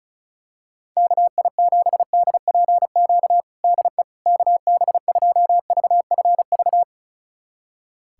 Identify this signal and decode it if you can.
Morse code, heard as a keyed tone, about 36 wpm, 700 Hz. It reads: KI7DPQ DE KB2VFV
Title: USA Callsign DE Callsign - 6 Characters - 35wpm